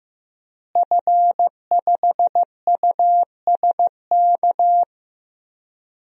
Morse code, heard as a keyed tone, about 15 wpm, 700 Hz. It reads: F5USK
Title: DXCC-100 Call Signs - 5 Characters - Repeated 2x - 15wpm